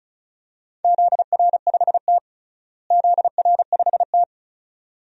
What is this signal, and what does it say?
Morse code, heard as a keyed tone, about 36 wpm, 700 Hz. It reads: ZR5T ZR5T
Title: DXCC-100 Call Signs - 4 Characters - Repeated 2x - 35wpm